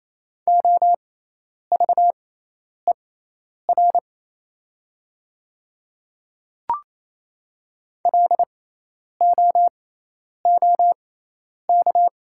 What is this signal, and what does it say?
Morse code, heard as a keyed tone, about 28 wpm, spaced as for 10 wpm, 690 Hz. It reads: OVER LOOK
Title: 6x Character Spacing - Top 200 Words - 28wpm